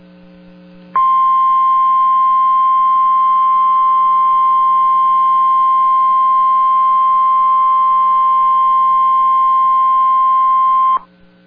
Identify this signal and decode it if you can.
Emergency Alert System (SAME) tones (the 1050 Hz attention tone). It is heard from 0.9 to 11.0 s.